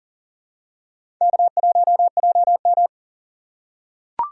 Morse code, heard as a keyed tone, about 38 wpm, 700 Hz. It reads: K1JM
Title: USA Call Signs - 4 Characters - 3x Character Spacing - 40wpm